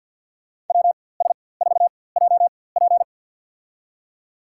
Morse code, heard as a keyed tone, about 49 wpm, spaced as for 23 wpm, 700 Hz. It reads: WS4JP
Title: USA Call Signs - 5 Characters - 4x Character Spacing - 50wpm